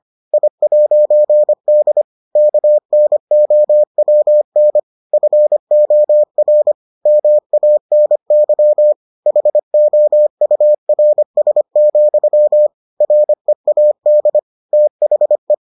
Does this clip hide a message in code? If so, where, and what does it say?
Yes: Morse code from 0.3 to 15.5 s, heard as a keyed tone, about 25 wpm, 600 Hz. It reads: I'D KNOWN FOR MANY HOURS, READ THE